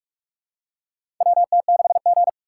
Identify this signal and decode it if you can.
Morse code, heard as a keyed tone, about 46 wpm, 700 Hz. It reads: WT6G
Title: USA Call Signs - 2x Character Spacing - 45wpm